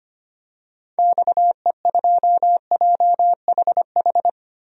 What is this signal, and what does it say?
Morse code, heard as a keyed tone, about 25 wpm, 700 Hz. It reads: XE2JHH
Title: DXCC-100 Call Signs - 6 Characters - 25wpm